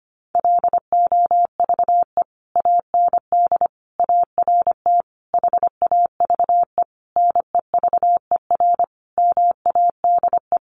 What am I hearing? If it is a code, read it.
Morse code, heard as a keyed tone, about 25 wpm, 700 Hz. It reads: LOVE AND ART HAVE NEVER MADE
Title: Sentences from Top 1100 Words 25wpm